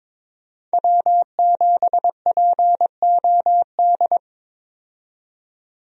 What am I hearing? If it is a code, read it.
Morse code, heard as a keyed tone, about 22 wpm, 700 Hz. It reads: W7POD
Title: USA Call Signs - 5 Characters - 22wpm